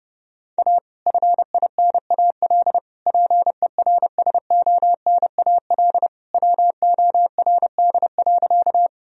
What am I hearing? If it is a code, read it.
Morse code, heard as a keyed tone, about 30 wpm, 700 Hz. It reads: A FINAL PERSONAL WORD.
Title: Sentences from Top 1000 Words - Review Part 4 - 30wpm